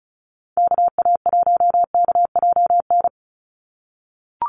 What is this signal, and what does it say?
Morse code, heard as a keyed tone, about 34 wpm, 700 Hz. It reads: KA1KJN
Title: USA Call Signs 35wpm